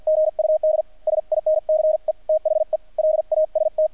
Morse code, heard as a keyed tone, about 50 wpm, spaced as for 44 wpm, 620 Hz, with radio noise. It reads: GUD SINCE THE LAST
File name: archived / CRSnet